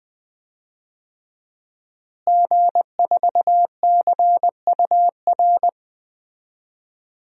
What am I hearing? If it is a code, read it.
Morse code, heard as a keyed tone, about 20 wpm, 700 Hz. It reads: G4CUR
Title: DXCC-100 Call Signs - 5 Characters - Repeated 2x - 20wpm